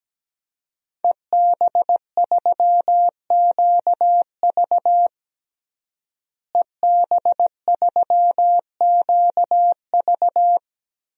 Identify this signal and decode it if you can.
Morse code, heard as a keyed tone, about 17 wpm, 700 Hz. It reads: EB3QV EB3QV
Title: DXCC-100 Call Signs - 5 Characters - Repeated 3x - 17wpm